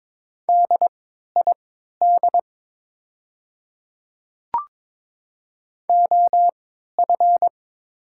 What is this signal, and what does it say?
Morse code, heard as a keyed tone, about 22 wpm, spaced as for 12 wpm, 690 Hz. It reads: DID OF
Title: 3x Character Spacing - Top 100 Words - 22wpm